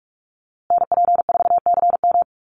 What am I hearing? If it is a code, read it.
Morse code, heard as a keyed tone, about 44 wpm, 690 Hz. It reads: NP4CM
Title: USA Call Signs Repeated 3x 45wpm